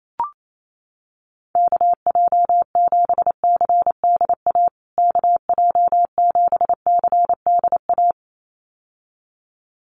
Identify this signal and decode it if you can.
Morse code, heard as a keyed tone, about 28 wpm, 700 Hz. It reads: KJ7CDA KJ7CDA
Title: USA Call Signs Repeated 2x 28wpm